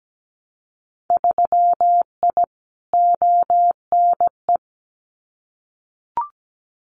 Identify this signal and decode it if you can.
Morse code, heard as a keyed tone, about 17 wpm, 700 Hz. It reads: 3I ONE
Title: Winter Field Day Exchanges Encore 17wpm